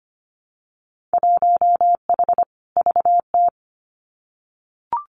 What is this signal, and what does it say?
Morse code, heard as a keyed tone, about 25 wpm, 700 Hz. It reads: 1H VT
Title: Winter Field Day Exchanges Encore 25wpm